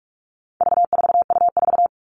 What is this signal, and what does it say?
Morse code, heard as a keyed tone, about 44 wpm, 700 Hz. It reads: V4U4
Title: All Cut Numbers - 4-Digit 45wpm